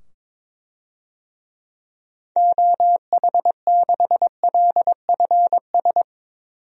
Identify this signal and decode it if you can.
Morse code, heard as a keyed tone, about 22 wpm, 700 Hz. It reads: OH6LFS
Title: DXCC-100 Call Signs - 6 Characters - 22wpm